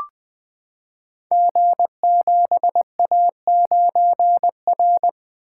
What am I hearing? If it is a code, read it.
Morse code, heard as a keyed tone, about 20 wpm, 700 Hz. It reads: G7A9R
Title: Code Groups - 5 Letter:Numbers - 20wpm